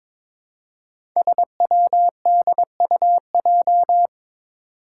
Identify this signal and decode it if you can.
Morse code, heard as a keyed tone, about 22 wpm, 700 Hz. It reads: SWDUJ
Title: Five Challenging Letter-Number Combinations 22wpm